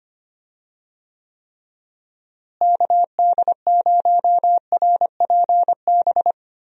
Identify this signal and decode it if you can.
Morse code, heard as a keyed tone, about 25 wpm, 700 Hz. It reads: KD0RPB
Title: USA Call Signs - 6 Characters - Repeated 3x - 25wpm